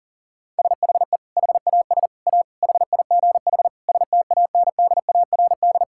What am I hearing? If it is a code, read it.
Morse code, heard as a keyed tone, about 40 wpm, 700 Hz. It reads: SHE HAS A HIGH STANDARD
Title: Sentences from Top 1000 Words - Review Part 6 - 40wpm